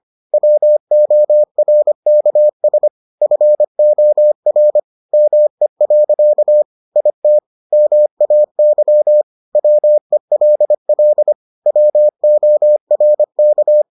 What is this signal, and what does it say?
Morse code, heard as a keyed tone, about 25 wpm, 600 Hz. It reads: WORKS FOR ME. IT MAY WELL WORK